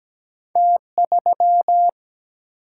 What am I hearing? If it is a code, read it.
Morse code, heard as a keyed tone, about 17 wpm, 700 Hz. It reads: T 3
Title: Single-Letter Single-Number 17wpm